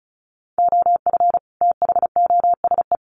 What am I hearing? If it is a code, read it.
Morse code, heard as a keyed tone, about 34 wpm, 700 Hz. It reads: OF THOSE
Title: Sets of 2 Words 35wpm